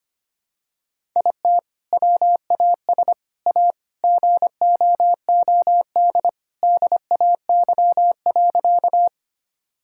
Morse code, heard as a keyed tone, about 25 wpm, 700 Hz. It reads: IT WAS A GOOD DAY.
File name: practice / 25wpm / Sentences with 2-Word Phrases - 25wpm